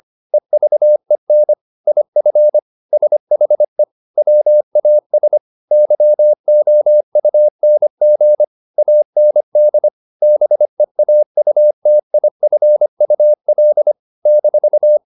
Morse code, heard as a keyed tone, about 25 wpm, 600 Hz. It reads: EVEN IF SHE WAS YOUNG AND BEAUTIFUL -